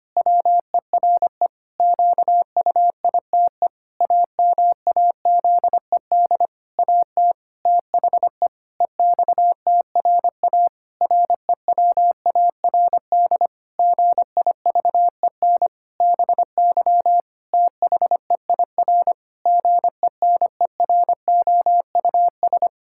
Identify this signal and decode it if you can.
Morse code, heard as a keyed tone, about 25 wpm, 700 Hz. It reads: WERE QUITE AMAZED AT THE EXTRA REWARD GIVEN BY THEIR GENEROUS
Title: Pangram Challenge v2 - 25wpm